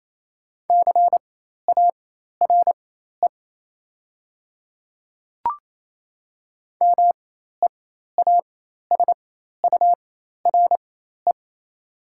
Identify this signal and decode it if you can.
Morse code, heard as a keyed tone, about 28 wpm, spaced as for 13 wpm, 700 Hz. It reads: CARE MEASURE
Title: 4x Character Spacing - Top 500 Words - 28wpm